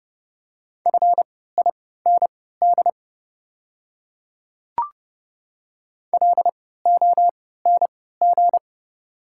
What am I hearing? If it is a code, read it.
Morse code, heard as a keyed tone, about 30 wpm, spaced as for 17 wpm, 700 Hz. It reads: FIND LONG
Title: 3x Character Spacing - Top 100 Words - 30wpm